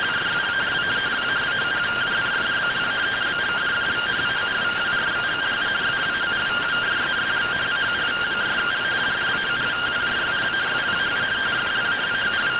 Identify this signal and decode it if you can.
Two-tone teleprinter signal, about 170 Hz shift.
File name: findings / CT AXIAL -572mm 2.gen-05 rtty rx